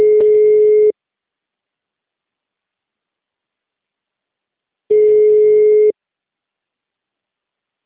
Ringback tone (Europe, 425 Hz).
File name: out- -2005- - - .454